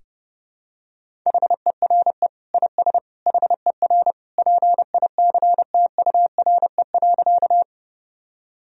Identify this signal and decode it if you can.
Morse code, heard as a keyed tone, about 30 wpm, 700 Hz. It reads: HERE IS HER PICTURE.
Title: Sentences from Top 1000 Words - Review Part 1 - 30wpm